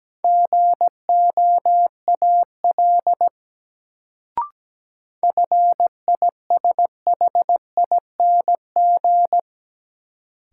Morse code, heard as a keyed tone, about 17 wpm, 700 Hz. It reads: GOAL FISHING
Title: Sentences from Top 1000 Words - Review Part 5 - New Word Drill - 17wpm